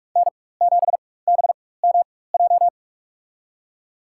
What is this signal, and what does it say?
Morse code, heard as a keyed tone, about 46 wpm, spaced as for 21 wpm, 700 Hz. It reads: N7BMJ
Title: USA Call Signs - 5 Characters - 4x Character Spacing - 45wpm